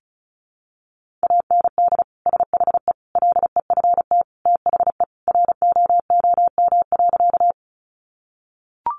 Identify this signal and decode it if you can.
Morse code, heard as a keyed tone, about 34 wpm, 700 Hz. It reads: AND SHE LEFT THE ROOM.
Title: Sentences from Top 300 Words 35wpm